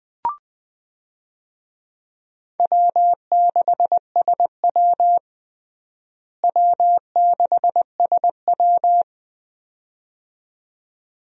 Morse code, heard as a keyed tone, about 20 wpm, 700 Hz. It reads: W6SW W6SW
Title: USA Call Signs - 4 Characters - Repeated 2x - 20wpm